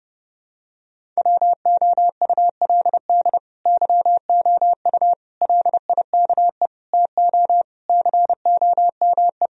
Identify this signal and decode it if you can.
Morse code, heard as a keyed tone, about 32 wpm, 700 Hz. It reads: WOULD YOU LIKE TO COME
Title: Sentences from Top 1000 Words - Review Part 1 - 30wpm